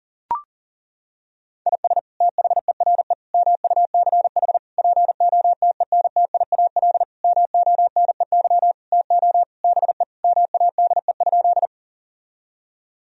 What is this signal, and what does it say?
Morse code, heard as a keyed tone, about 37 wpm, 700 Hz. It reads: IS THERE MUCH POTENTIAL MONEY TO BE MADE?
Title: Sentences from Top 1000 Words - Review Part 4 - 40wpm